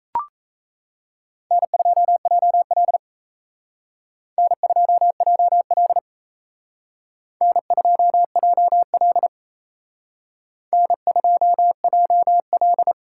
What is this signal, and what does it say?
Morse code, heard as a keyed tone, about 34 wpm, 700 Hz. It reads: N2JL N2JL N2JL N2JL
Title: Speed Racing - USA 1x2 Call Signs 28wpm